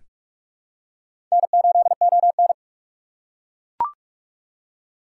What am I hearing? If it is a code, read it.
Morse code, heard as a keyed tone, about 42 wpm, 700 Hz. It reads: N8ON
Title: USA Call Signs - 4 Characters - 6x Character Spacing - 45wpm